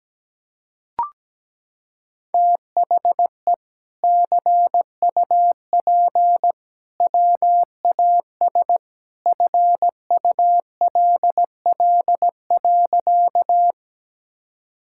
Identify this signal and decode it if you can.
Morse code, heard as a keyed tone, about 17 wpm, 700 Hz. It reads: THE CUP WAS FULL.
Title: Sentences from Top 1600 Words 17wpm